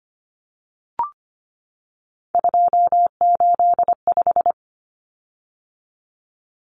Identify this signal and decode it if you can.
Morse code, heard as a keyed tone, about 25 wpm, 700 Hz. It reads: 285
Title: Common Cut Numbers - 3-Digit 25wpm